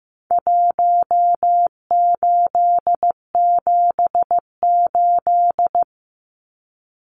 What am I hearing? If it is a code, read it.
Morse code, heard as a keyed tone, about 15 wpm, 700 Hz. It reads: 1878
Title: Numbers - Contest Style - 4-Digit 15wpm